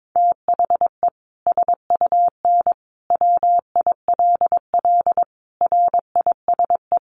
Morse code, heard as a keyed tone, about 22 wpm, 700 Hz. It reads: THE SUN WILL RISE
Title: Sentences from Top 900 Words 22wpm